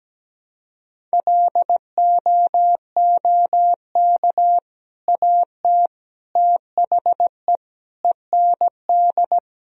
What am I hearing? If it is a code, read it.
Morse code, heard as a keyed tone, about 17 wpm, 700 Hz. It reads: LOOK AT THE END
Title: Sentences from Top 1000 Words - Review Part 1 - 17wpm